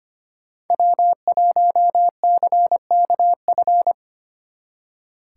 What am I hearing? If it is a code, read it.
Morse code, heard as a keyed tone, about 31 wpm, 700 Hz. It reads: W1CKF
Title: Speed Racing - USA 1x3 Call Signs 25wpm